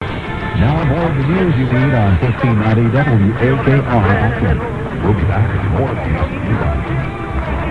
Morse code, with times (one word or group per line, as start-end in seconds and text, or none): none